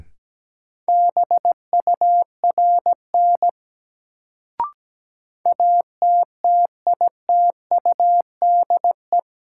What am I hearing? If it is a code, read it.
Morse code, heard as a keyed tone, about 17 wpm, 700 Hz. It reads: BURN ATTITUDE
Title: Sentences from Top 1300 Words - New Word Drill Alt - 17wpm